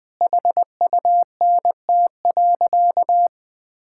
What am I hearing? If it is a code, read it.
Morse code, heard as a keyed tone, about 21 wpm, 700 Hz. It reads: HUNT.
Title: Sentences from Top 1600 Words 20wpm